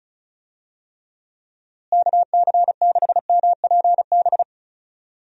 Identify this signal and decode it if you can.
Morse code, heard as a keyed tone, about 34 wpm, 700 Hz. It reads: KC6MPB
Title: USA Call Signs - 6 Characters - Repeated 3x - 35wpm